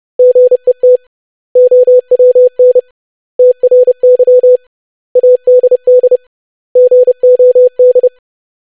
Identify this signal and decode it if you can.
Morse code, heard as a keyed tone, about 30 wpm, 500 Hz. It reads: GET OWN TRY ADD GOD
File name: lesson / web / basic500-25wpm